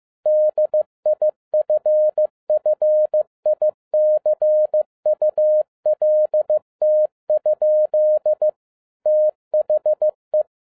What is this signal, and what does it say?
Morse code, heard as a keyed tone, about 15 wpm, 610 Hz. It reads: DIFFICULT? THE